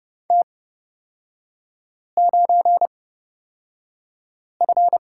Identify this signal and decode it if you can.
Morse code, heard as a keyed tone, about 30 wpm, spaced as for 5 wpm, 700 Hz. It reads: T9F
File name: Lesson 36e - BT - Three Characters 30wpm